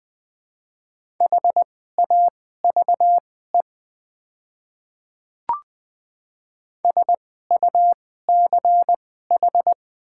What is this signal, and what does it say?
Morse code, heard as a keyed tone, about 20 wpm, spaced as for 15 wpm, 700 Hz. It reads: HAVE SUCH
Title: 2x Character Spacing - Top 200 Words - 20wpm